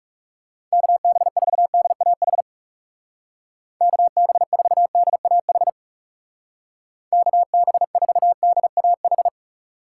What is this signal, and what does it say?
Morse code, heard as a keyed tone, about 38 wpm, 700 Hz. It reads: KB4DAH KB4DAH KB4DAH